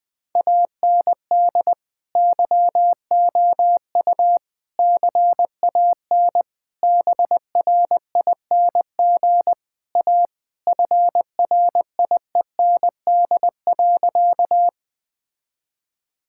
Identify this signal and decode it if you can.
Morse code, heard as a keyed tone, about 20 wpm, 700 Hz. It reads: AND YOU CAN BRING A FRIEND.
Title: Sentences with 2-Word Phrases - Spoken Beginning - 20wpm